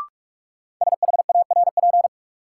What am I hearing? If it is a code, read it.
Morse code, heard as a keyed tone, about 46 wpm, 700 Hz. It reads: SHARP
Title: Sentences from Top 1600 Words - New Word Drill Alt - 45wpm